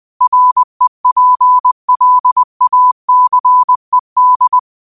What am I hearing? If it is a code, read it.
Morse code, heard as a keyed tone, about 20 wpm, 1000 Hz. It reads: REPLACED
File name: qotdfiles / 20wpm